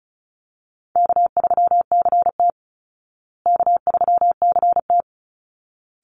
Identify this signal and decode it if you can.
Morse code, heard as a keyed tone, about 34 wpm, 700 Hz. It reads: K3CT K3CT
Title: Call Signs Repeated 3x - 35wpm